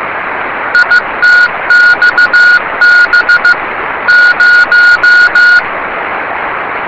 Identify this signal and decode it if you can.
Morse code, heard as a keyed tone, about 15 wpm, 1440 Hz. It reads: ITXB 0
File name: yubileiry cw by dk3wn